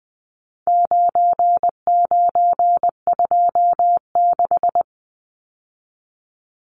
Morse code, heard as a keyed tone, about 20 wpm, 700 Hz. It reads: 9926
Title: Numbers - 4-Digit 20wpm